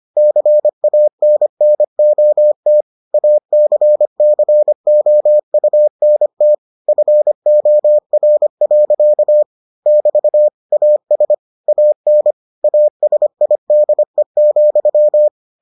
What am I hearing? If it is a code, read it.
Morse code, heard as a keyed tone, about 25 wpm, 600 Hz. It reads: CANNOT ACCOUNT FOR. <BT> AS AN ASIDE,